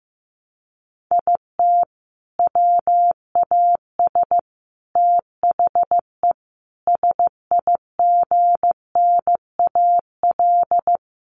Morse code, heard as a keyed tone, about 15 wpm, 700 Hz. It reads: IT WAS THE SIGNAL
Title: Sentences from Top 1400 Words 15wpm